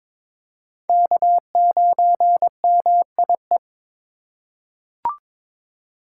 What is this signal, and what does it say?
Morse code, heard as a keyed tone, about 22 wpm, 700 Hz. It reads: K9MIE
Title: USA Call Signs - 5 Characters - 3x Character Spacing - 22wpm